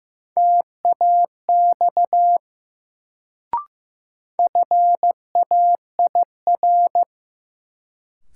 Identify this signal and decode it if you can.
Morse code, heard as a keyed tone, about 15 wpm, 700 Hz. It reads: TAX FAIR
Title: Sentences from Top 1000 Words - Review Part 5 - New Word Drill - 15wpm